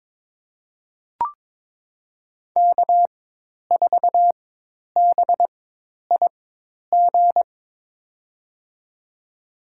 Morse code, heard as a keyed tone, about 22 wpm, spaced as for 10 wpm, 700 Hz. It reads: K4BIG
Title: USA Call Signs - 5 Characters - 4x Character Spacing - 22wpm